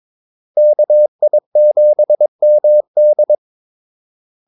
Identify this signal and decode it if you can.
Morse code, heard as a keyed tone, about 22 wpm, 600 Hz. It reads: KI7MD
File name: Callsigns-22wpm